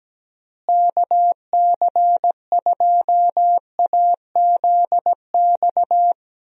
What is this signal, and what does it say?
Morse code, heard as a keyed tone, about 17 wpm, 700 Hz. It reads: KC2AZX
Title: USA Call Signs - 6 Characters - Repeated 3x - 17wpm